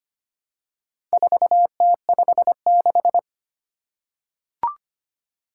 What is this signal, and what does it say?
Morse code, heard as a keyed tone, about 25 wpm, 700 Hz. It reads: 4T56
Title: Four Challenging Letter-Number Combinations 25wpm